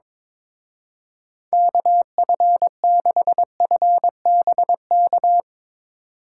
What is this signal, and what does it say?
Morse code, heard as a keyed tone, about 22 wpm, 700 Hz. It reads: KF6FBK
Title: Speed Racing - USA 2x3 Call Signs 22wpm